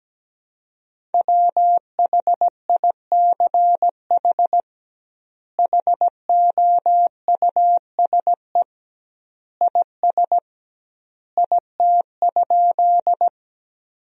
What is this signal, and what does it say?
Morse code, heard as a keyed tone, about 17 wpm, 700 Hz. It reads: WHICH HOUSE IS IT?
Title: 2x Word-Spacing - Sentences from Top 200 Words - 17wpm